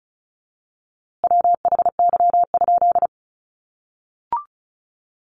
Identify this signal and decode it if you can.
Morse code, heard as a keyed tone, about 34 wpm, 700 Hz. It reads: WHY?
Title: A Sound Of Thunder 35wpm